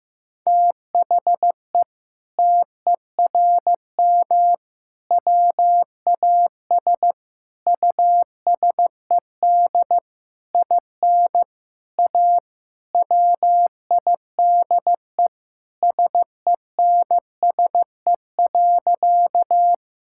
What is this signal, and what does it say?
Morse code, heard as a keyed tone, about 15 wpm, 700 Hz. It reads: THE TERM WAS USED IN A WIDE SENSE.
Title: Sentences from Top 1000 Words - Review Part 6 - 15wpm